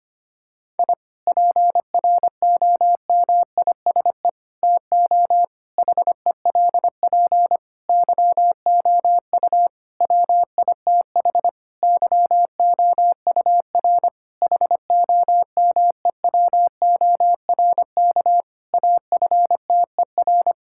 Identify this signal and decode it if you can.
Morse code, heard as a keyed tone, about 25 wpm, 700 Hz. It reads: I PROMISE TO HELP YOU WITH YOUR HOMEWORK AFTER
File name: practice / 25wpm / Difficult Sentences with Hints - 25wpm